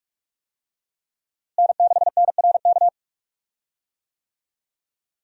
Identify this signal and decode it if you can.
Morse code, heard as a keyed tone, about 44 wpm, 700 Hz. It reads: N6NRK
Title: USA Call Signs - 5 Characters - Repeated 2x - 45wpm